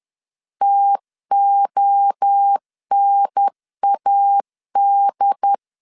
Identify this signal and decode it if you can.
Morse code, heard as a keyed tone, about 10 wpm, 800 Hz. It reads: TONAD
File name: CW / DET / 1 3 1500C S C 10wpm 3335